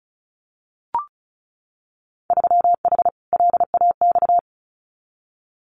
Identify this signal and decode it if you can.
Morse code, heard as a keyed tone, about 36 wpm, 690 Hz. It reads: 3H LAX
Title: Winter Field Day Exchanges 35wpm